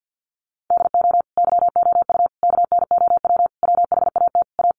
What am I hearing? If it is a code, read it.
Morse code, heard as a keyed tone, about 50 wpm, 700 Hz. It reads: DO YOU KNOW WHAT A
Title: Sentences from Top 1500 Words 50wpm